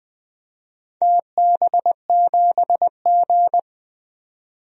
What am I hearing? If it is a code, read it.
Morse code, heard as a keyed tone, about 20 wpm, 700 Hz. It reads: TB7G
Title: DXCC-100 Call Signs - 4 Characters - 20wpm